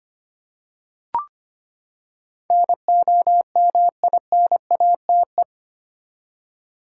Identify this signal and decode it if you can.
Morse code, heard as a keyed tone, about 25 wpm, 700 Hz. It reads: NOMINATE
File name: Eight Letter Words - 25wpm